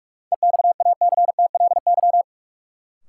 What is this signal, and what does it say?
Morse code, heard as a keyed tone, about 42 wpm, 700 Hz. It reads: EXACTLY
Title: Sentences from Top 900 Words - New Word Drill - 45wpm